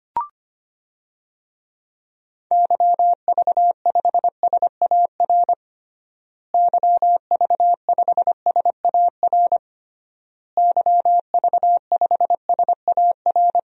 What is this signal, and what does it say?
Morse code, heard as a keyed tone, about 25 wpm, 700 Hz. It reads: YV5SAR YV5SAR YV5SAR
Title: DXCC-100 Call Signs - 6 Characters - Repeated 3x - 25wpm